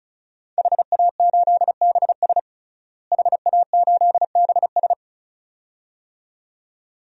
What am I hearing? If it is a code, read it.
Morse code, heard as a keyed tone, about 36 wpm, 700 Hz. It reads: HA8BS HA8BS
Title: DXCC-100 Call Signs - 5 Characters - Repeated 2x - 35wpm